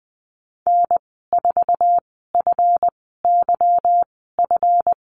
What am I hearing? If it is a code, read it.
Morse code, heard as a keyed tone, about 20 wpm, spaced as for 14 wpm, 700 Hz. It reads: N4FYF
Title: 207 Lesson 33g - Letter j - Call Signs 20wpm